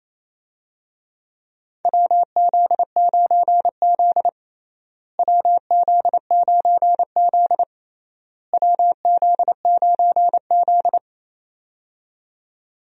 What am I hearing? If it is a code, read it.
Morse code, heard as a keyed tone, about 28 wpm, 700 Hz. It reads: WZ9Z WZ9Z WZ9Z
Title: USA Call Signs - 4 Characters - Repeated 3x - 28wpm